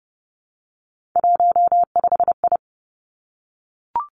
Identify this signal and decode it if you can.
Morse code, heard as a keyed tone, about 32 wpm, 700 Hz. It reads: 15I
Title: Three Challenging Letter-Number Combinations 30wpm